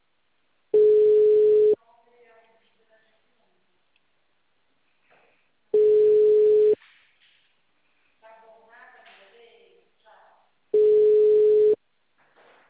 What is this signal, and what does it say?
Ringback tone (Europe, 425 Hz), with speech.
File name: out- -2005- - - .36